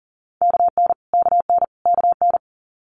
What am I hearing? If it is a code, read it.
Morse code, heard as a keyed tone, about 42 wpm, 690 Hz. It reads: KN KN KN
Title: Multi-letter Phonemes Repeated 3x 40wpm